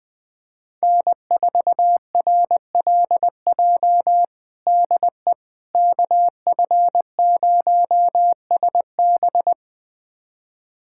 Morse code, heard as a keyed tone, about 20 wpm, 700 Hz. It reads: N4RLJ DE KF0SB
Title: USA Callsign DE Callsign - 5 Characters - 20wpm